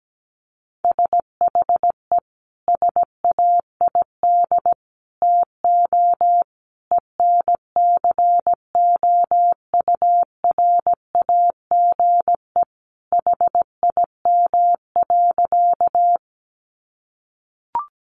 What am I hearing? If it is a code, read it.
Morse code, heard as a keyed tone, about 17 wpm, 700 Hz. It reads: SHE SAID TO ENCOURAGE HIM.
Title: Sentences from Top 1400 Words 17wpm